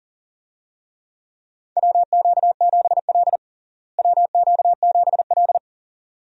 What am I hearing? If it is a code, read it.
Morse code, heard as a keyed tone, about 40 wpm, 700 Hz. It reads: WQ7L WQ7L
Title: USA Call Signs - 4 Characters - Repeated 2x - 40wpm